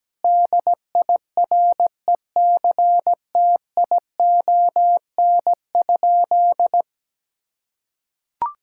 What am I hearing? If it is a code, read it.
Morse code, heard as a keyed tone, about 17 wpm, 700 Hz. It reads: DIRECTION?
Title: Sentences from Top 1000 Words - Review Part 5 - 17wpm